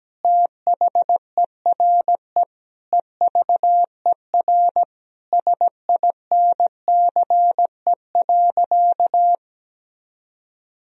Morse code, heard as a keyed tone, about 17 wpm, 700 Hz. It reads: THERE EVER SINCE.
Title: Sentences from Top 400 Words and Abbreviations 17wpm